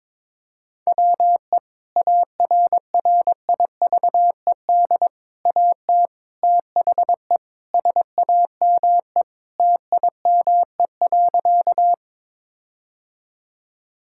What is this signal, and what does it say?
Morse code, heard as a keyed tone, about 22 wpm, 700 Hz. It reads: WE ARRIVED AT THE SAME TIME.
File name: practice / 22wpm / Sentences with 4-Word Phrases - 22wpm